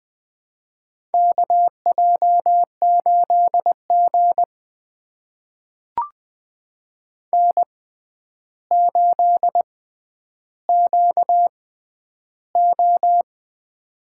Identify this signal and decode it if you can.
Morse code, heard as a keyed tone, about 20 wpm, spaced as for 10 wpm, 700 Hz. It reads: KJ8G N8QO
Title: USA Call Signs - 4 Characters - 6x Character Spacing - 20wpm